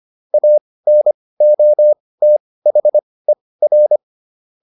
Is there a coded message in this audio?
Yes: Morse code from 0.3 to 4.0 s, heard as a keyed tone, about 25 wpm, spaced as for 18 wpm, 600 Hz. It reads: ANOTHER